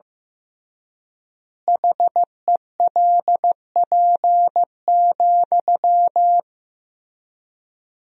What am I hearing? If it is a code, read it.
Morse code, heard as a keyed tone, about 14 wpm, 700 Hz. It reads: HELP,
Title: 3x Word-Spacing - Sentences from Top 300 Words - 15wpm